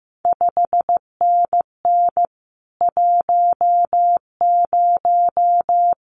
Morse code, heard as a keyed tone, about 15 wpm, 700 Hz. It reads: 5NN 10
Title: CQ WW DX Contest Exchange Encore - 15wpm